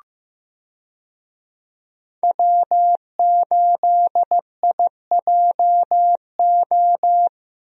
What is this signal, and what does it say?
Morse code, heard as a keyed tone, about 15 wpm, 700 Hz. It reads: W8IJO
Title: USA Call Signs - 5 Characters - Repeated 3x - 15wpm